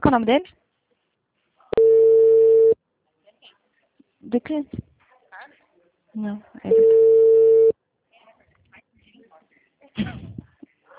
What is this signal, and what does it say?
Ringback tone (China, 450 Hz), with speech.
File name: out- -2012